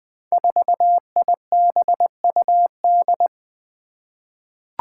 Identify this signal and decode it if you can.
Morse code, heard as a keyed tone, about 20 wpm, 700 Hz. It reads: 4IBUD
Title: Five Challenging Letter-Number Combinations 20wpm